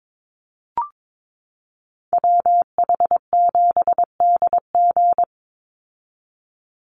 Morse code, heard as a keyed tone, about 22 wpm, 700 Hz. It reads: WH7DG
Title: USA Call Signs 22wpm